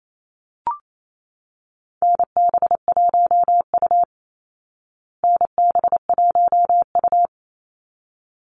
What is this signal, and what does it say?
Morse code, heard as a keyed tone, about 28 wpm, 700 Hz. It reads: NB1U NB1U
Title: Call Signs Repeated 3x - 28wpm